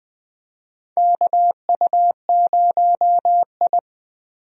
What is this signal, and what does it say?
Morse code, heard as a keyed tone, about 20 wpm, 700 Hz. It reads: KU0I
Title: USA Call Signs - 4 Characters - 20wpm